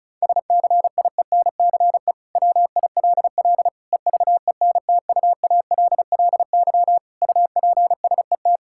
Morse code, heard as a keyed tone, about 34 wpm, 700 Hz. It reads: SCIENCE WILL EVENTUALLY UPSET
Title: Sentences from Top 1000 Words - Review Part 6 - 35wpm